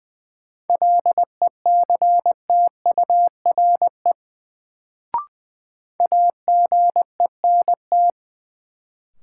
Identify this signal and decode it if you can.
Morse code, heard as a keyed tone, about 20 wpm, 700 Hz. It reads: LECTURE AGENT
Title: Sentences from Top 1600 Words - New Word Drill - 20wpm